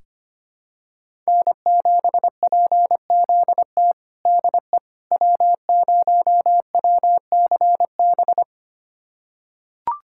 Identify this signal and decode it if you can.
Morse code, heard as a keyed tone, about 25 wpm, 700 Hz. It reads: N7PZT DE W0WCB
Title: USA Callsign DE Callsign 25wpm